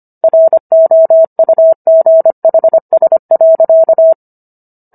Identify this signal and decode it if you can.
Morse code, heard as a keyed tone, about 25 wpm, 650 Hz. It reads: ROUGHS.